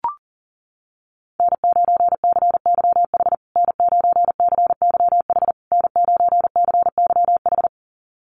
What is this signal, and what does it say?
Morse code, heard as a keyed tone, about 40 wpm, 700 Hz. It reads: N9CYH N9CYH N9CYH
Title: USA Call Signs Repeated 3x 40wpm